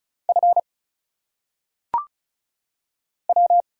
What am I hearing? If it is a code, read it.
Morse code, heard as a keyed tone, about 34 wpm, 700 Hz. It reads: F W
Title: Single Letters - Phonetic Pronunciation 35wpm